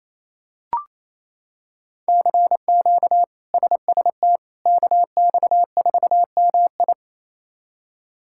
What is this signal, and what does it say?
Morse code, heard as a keyed tone, about 28 wpm, 700 Hz. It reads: CQ SST KX4MI
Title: K1USN SST QSO Practice - 28wpm